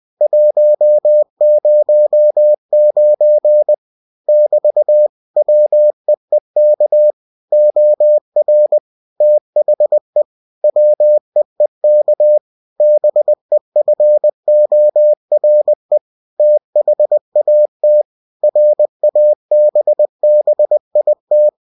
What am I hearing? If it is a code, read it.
Morse code, heard as a keyed tone, about 20 wpm, 600 Hz. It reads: 109 <BT> WEEK OR THE WEEK BEFORE THAT RABBIT
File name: cw-books / winnie-the-pooh / winnie-20wpm-page-0109